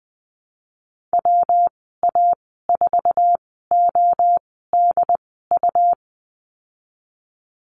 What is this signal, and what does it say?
Morse code, heard as a keyed tone, about 20 wpm, spaced as for 15 wpm, 700 Hz. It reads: WA4ODU
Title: 140 Lesson 23g - Letter f - Call Signs 20wpm